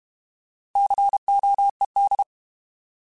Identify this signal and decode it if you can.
Morse code, heard as a keyed tone, about 32 wpm, 800 Hz. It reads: COED
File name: Alphabetical / Alphabetical mp3s / Cs @30